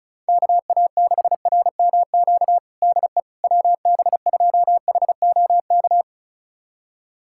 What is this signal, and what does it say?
Morse code, heard as a keyed tone, about 36 wpm, 700 Hz. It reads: KA6RMQ DE WB2HOK
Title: USA Callsign DE Callsign - 6 Characters - 35wpm